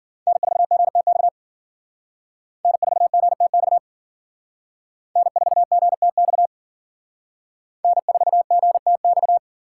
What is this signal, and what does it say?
Morse code, heard as a keyed tone, about 53 wpm, 700 Hz. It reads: N4GTX N4GTX N4GTX N4GTX